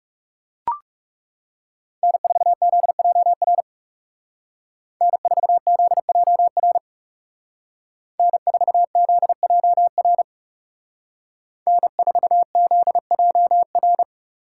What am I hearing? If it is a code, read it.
Morse code, heard as a keyed tone, about 35 wpm, 700 Hz. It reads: N4ZJR N4ZJR N4ZJR N4ZJR
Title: Speed Racing - USA 1x3 Call Signs 30wpm